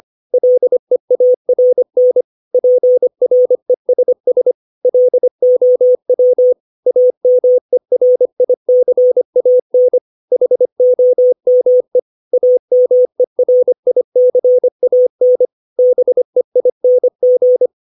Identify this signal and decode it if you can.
Morse code, heard as a keyed tone, about 25 wpm, 500 Hz. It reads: LEARN PRESS LOW AMERICAN HOME AMERICAN BEING
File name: 632 words / 632 Words 25WPM